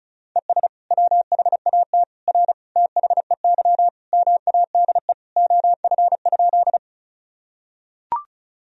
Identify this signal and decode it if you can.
Morse code, heard as a keyed tone, about 36 wpm, 700 Hz. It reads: ES WHAT R THEY MADE OF?
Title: Sentences from Top 200 Words and Abbreviations 35wpm